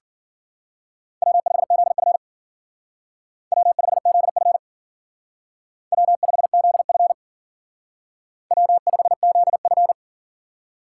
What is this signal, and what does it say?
Morse code, heard as a keyed tone, about 49 wpm, 700 Hz. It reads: W5ZF W5ZF W5ZF W5ZF
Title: Speed Racing - USA 1x2 Call Signs 40wpm